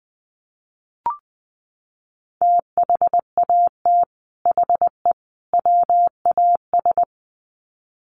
Morse code, heard as a keyed tone, about 20 wpm, 700 Hz. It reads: THAT HE WAS
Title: Sets of 3 Words 20wpm